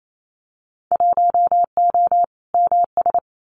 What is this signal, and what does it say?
Morse code, heard as a keyed tone, about 28 wpm, 700 Hz. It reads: 1O MS
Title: Winter Field Day Exchanges 28wpm